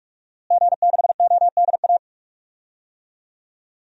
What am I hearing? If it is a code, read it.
Morse code, heard as a keyed tone, about 44 wpm, 700 Hz. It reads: G6ODA
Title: DXCC-100 Call Signs - 5 Characters - Repeated 2x - 45wpm